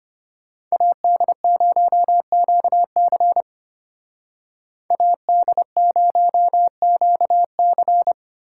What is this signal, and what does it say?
Morse code, heard as a keyed tone, about 31 wpm, 700 Hz. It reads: AD0QC AD0QC
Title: Speed Racing - USA 2x2 Call Signs 25wpm